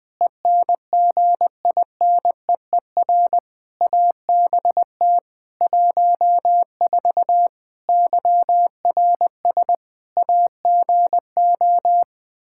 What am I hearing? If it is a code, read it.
Morse code, heard as a keyed tone, about 20 wpm, 700 Hz. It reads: ENGINEER ABT 14 YRS AGO
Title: K1USN SST QSO Practice 20wpm